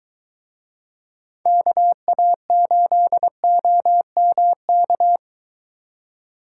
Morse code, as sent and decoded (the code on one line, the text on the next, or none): -.- .- ---.. --- -- -.-
KA8OMK